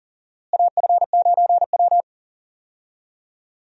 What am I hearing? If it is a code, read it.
Morse code, heard as a keyed tone, about 40 wpm, 700 Hz. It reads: AF9W
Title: USA Call Signs - 4 Characters - Repeated 2x - 40wpm